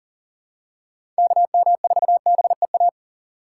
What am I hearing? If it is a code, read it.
Morse code, heard as a keyed tone, about 37 wpm, 700 Hz. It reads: KM4BEA
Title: USA Call Signs - 6 Characters - 4x Character Spacing - 40wpm